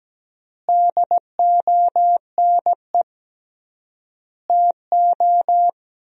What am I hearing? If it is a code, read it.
Morse code, heard as a keyed tone, about 17 wpm, 700 Hz. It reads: DONE TO
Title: 3x Word-Spacing - Sentences from Top 500 Words - 17wpm